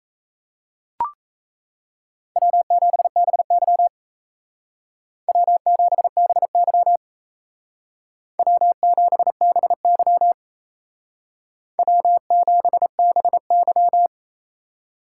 Morse code, heard as a keyed tone, about 37 wpm, 700 Hz. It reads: W7BY W7BY W7BY W7BY